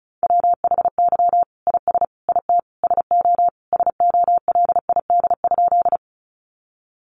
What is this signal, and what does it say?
Morse code, heard as a keyed tone, about 34 wpm, 700 Hz. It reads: WHY IS IT SO SOLID?
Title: Sentences from Top 1500 Words 35wpm